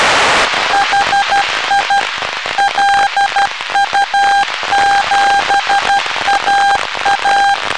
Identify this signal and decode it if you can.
Morse code, heard as a keyed tone, about 12 wpm, 800 Hz. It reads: HI LU7AA
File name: lu4aao / globo19 1 LU2DPW